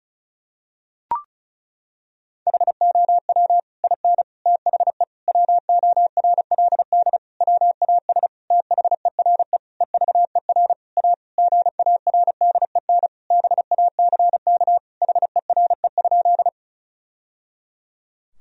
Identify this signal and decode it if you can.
Morse code, heard as a keyed tone, about 34 wpm, 700 Hz. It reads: HOW IN THE WORLD WAS THERE EVER A GARDEN BACK HERE?
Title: Sentences from Top 1000 Words - Review Part 4 - 35wpm